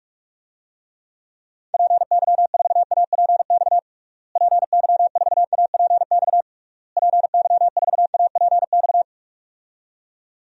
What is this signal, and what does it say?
Morse code, heard as a keyed tone, about 46 wpm, 700 Hz. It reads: PY4APX PY4APX PY4APX
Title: DXCC-100 Call Signs - 6 Characters - Repeated 3x - 45wpm